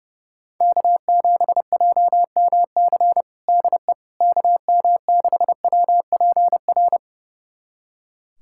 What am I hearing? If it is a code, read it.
Morse code, heard as a keyed tone, about 30 wpm, 700 Hz. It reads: K7JMC DE KM6WPR
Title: USA Callsign DE Callsign 30wpm